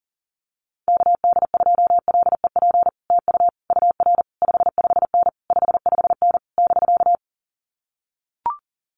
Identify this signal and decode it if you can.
Morse code, heard as a keyed tone, about 40 wpm, 700 Hz. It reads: KD2LEP TU UR 55N 55N <BK>
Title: Common QSO Phrases Encore 40wpm